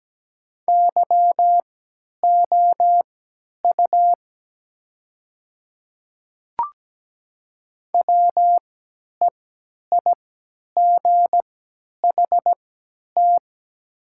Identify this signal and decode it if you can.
Morse code, heard as a keyed tone, about 17 wpm, spaced as for 10 wpm, 700 Hz. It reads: YOU WEIGHT
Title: 3x Character Spacing - Top 500 Words - 17wpm